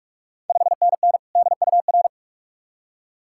Morse code, heard as a keyed tone, about 44 wpm, 700 Hz. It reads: 5NN DUR
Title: Ontario QSO Party Exchange 45wpm